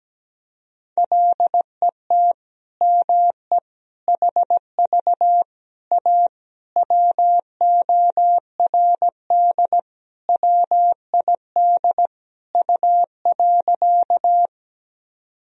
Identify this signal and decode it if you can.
Morse code, heard as a keyed tone, about 17 wpm, 700 Hz. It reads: LET ME HV A WORD WID U.
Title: Sentences from Top 300 Words and Abbreviations 17wpm